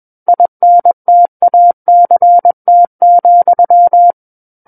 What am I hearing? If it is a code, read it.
Morse code, heard as a keyed tone, about 21 wpm, 700 Hz. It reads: INTACT,
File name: Book One / 12-Chapter 12 - 21 WPM